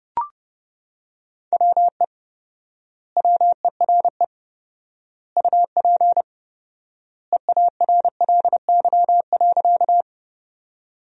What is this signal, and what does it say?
Morse code, heard as a keyed tone, about 30 wpm, 700 Hz. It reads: WE WERE UP EARLY.
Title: 4x Word-Spacing - Sentences from Top 400 Words - 30wpm